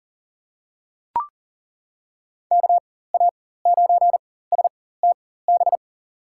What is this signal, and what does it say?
Morse code, heard as a keyed tone, about 40 wpm, spaced as for 19 wpm, 700 Hz. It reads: KA9STB
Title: USA Call Signs - 6 Characters - 4x Character Spacing - 40wpm